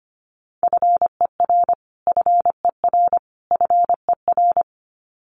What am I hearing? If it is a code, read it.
Morse code, heard as a keyed tone, about 25 wpm, 700 Hz. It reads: FER FER FER
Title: Abbreviations Repeated 3x - 25wpm